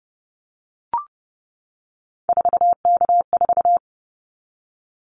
Morse code, heard as a keyed tone, about 32 wpm, 700 Hz. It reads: 4K4
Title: Callsign Prefixes for Top 100 DXCC - 30wpm